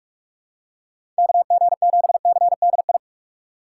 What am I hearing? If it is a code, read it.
Morse code, heard as a keyed tone, about 47 wpm, 700 Hz. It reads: KG7CDI
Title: USA Call Signs - 6 Characters - 4x Character Spacing - 45wpm